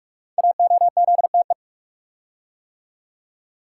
Morse code, heard as a keyed tone, about 48 wpm, 700 Hz. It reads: AOZTE
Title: Code Groups - 5 Letters - 45wpm